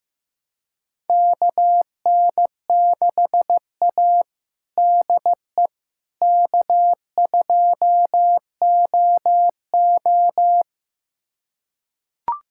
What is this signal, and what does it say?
Morse code, heard as a keyed tone, about 15 wpm, 700 Hz. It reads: KN6A DE K2OO
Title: USA Callsign DE Callsign - 4 Characters - 15wpm